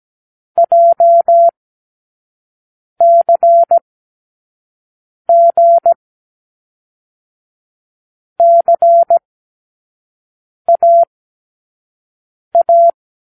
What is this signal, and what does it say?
Morse code, heard as a keyed tone, about 17 wpm, 680 Hz. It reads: J C G C A A
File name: k7qo / Lesson15(A-J)